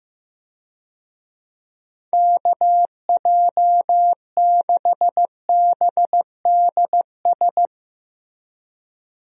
Morse code, heard as a keyed tone, about 15 wpm, 700 Hz. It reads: KJ6BDS
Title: USA Call Signs - 6 Characters - Repeated 2x - 15wpm